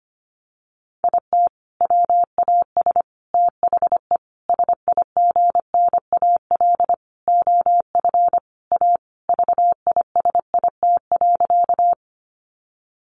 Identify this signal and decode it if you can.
Morse code, heard as a keyed tone, about 25 wpm, 700 Hz. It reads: IT WAS THE SIGNAL OF A VISIT.
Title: Sentences from Top 1400 Words 25wpm